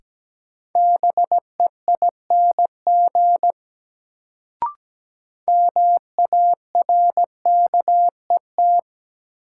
Morse code, heard as a keyed tone, about 17 wpm, 700 Hz. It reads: BEING MARKET
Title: Sentences from Top 600 Words - New Word Drill - 17wpm